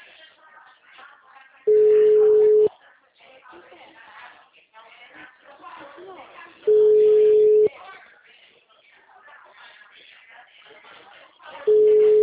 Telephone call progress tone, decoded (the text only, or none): ringback tone